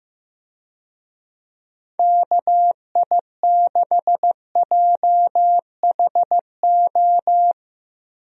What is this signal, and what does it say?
Morse code, heard as a keyed tone, about 15 wpm, 700 Hz. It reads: KI6JHO
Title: USA Call Signs - 6 Characters - Repeated 2x - 15wpm